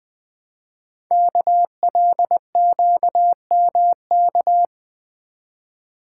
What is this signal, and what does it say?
Morse code, heard as a keyed tone, about 20 wpm, 700 Hz. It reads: KLQMK
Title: Code Groups - 5 Letters - 20wpm